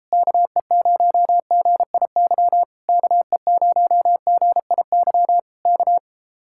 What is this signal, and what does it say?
Morse code, heard as a keyed tone, about 31 wpm, 700 Hz. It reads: KE0GIY KE0GIY K